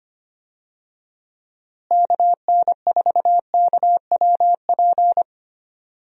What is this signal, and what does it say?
Morse code, heard as a keyed tone, about 25 wpm, 700 Hz. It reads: KN4KWP
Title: USA Call Signs - 6 Characters - Repeated 2x - 25wpm